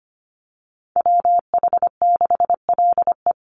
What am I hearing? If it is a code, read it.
Morse code, heard as a keyed tone, about 25 wpm, 700 Hz. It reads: WH6LE
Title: NA SOTA Activator Callsigns 25wpm